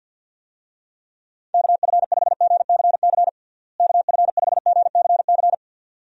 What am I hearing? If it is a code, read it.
Morse code, heard as a keyed tone, about 52 wpm, 700 Hz. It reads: KF5GCC KF5GCC
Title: USA Call Signs - 6 Characters - Repeated 2x - 50wpm